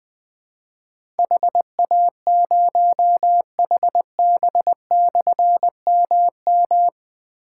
Morse code, heard as a keyed tone, about 21 wpm, 700 Hz. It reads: HA0HB/MM
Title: Difficult Call Signs 20wpm